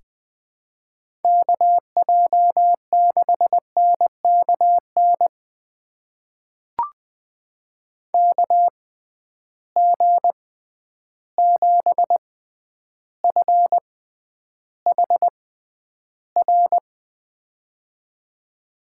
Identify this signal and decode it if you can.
Morse code, heard as a keyed tone, about 20 wpm, spaced as for 10 wpm, 700 Hz. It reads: KJ6NKN KG7FHR
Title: USA Call Signs - 6x Character Spacing - 20wpm